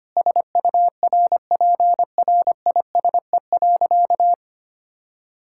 Morse code, heard as a keyed tone, about 25 wpm, 700 Hz. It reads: SURPRISE.
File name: practice / 25wpm / Sentences with 4-Word Phrases - 25wpm